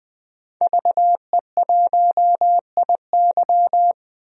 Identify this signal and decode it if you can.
Morse code, heard as a keyed tone, about 21 wpm, 700 Hz. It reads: VE1IY
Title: DXCC-100 Call Signs - 5 Characters - 20wpm